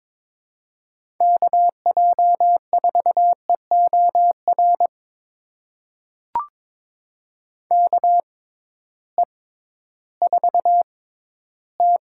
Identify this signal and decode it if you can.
Morse code, heard as a keyed tone, about 22 wpm, spaced as for 13 wpm, 700 Hz. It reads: KJ4EOR KE4T
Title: USA Call Signs - 6 Characters - 6x Character Spacing - 22wpm